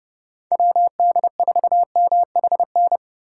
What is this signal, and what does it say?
Morse code, heard as a keyed tone, about 30 wpm, 700 Hz. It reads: WD4MHN